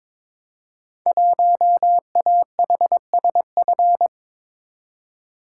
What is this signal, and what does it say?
Morse code, heard as a keyed tone, about 23 wpm, 700 Hz. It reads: 1AHSF
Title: Code Groups - 5 Letter:Numbers - 22wpm